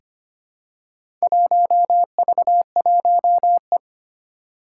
Morse code, heard as a keyed tone, about 25 wpm, 700 Hz. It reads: 1V1E
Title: Four Challenging Letter-Number Combinations 25wpm